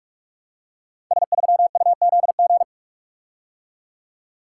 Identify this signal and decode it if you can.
Morse code, heard as a keyed tone, about 44 wpm, 700 Hz. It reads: S3UZG
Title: Code Groups - 5 Letter:Numbers - 45wpm